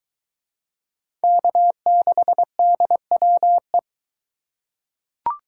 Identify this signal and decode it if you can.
Morse code, heard as a keyed tone, about 20 wpm, 700 Hz. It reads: K6DWE